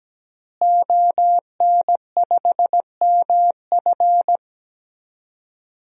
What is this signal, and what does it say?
Morse code, heard as a keyed tone, about 17 wpm, 700 Hz. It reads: ON5MF
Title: DXCC-100 Call Signs - 5 Characters - Repeated 3x - 17wpm